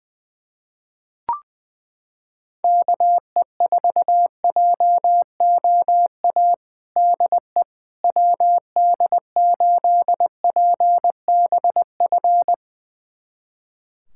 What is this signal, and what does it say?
Morse code, heard as a keyed tone, about 20 wpm, 700 Hz. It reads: KE4JOA DE WD8PBF
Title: USA Callsign DE Callsign - 6 Characters - 20wpm